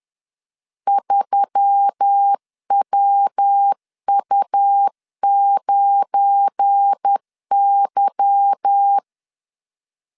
Morse code, heard as a keyed tone, about 10 wpm, 800 Hz. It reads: 3WU9Y
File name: LEZIONI / 8 YJ45 / 8 Y 10wpm 2615